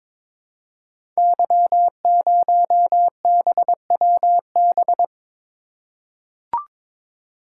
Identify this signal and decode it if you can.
Morse code, heard as a keyed tone, about 22 wpm, 700 Hz. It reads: Y0BWB
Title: Code Groups - 5 Letter:Numbers - 22wpm